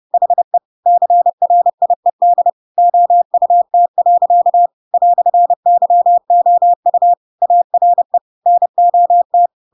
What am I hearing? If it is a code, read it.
Morse code, heard as a keyed tone, about 30 wpm, 700 Hz. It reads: HE CRIED OUT. "YOU ARE NOT
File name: Pinocchio / Pinocchio ch3 30 30 ik1hge